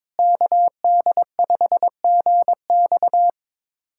Morse code, heard as a keyed tone, about 21 wpm, 700 Hz. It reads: KD5GX
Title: USA Callsign DE Callsign - 5 Characters - 22wpm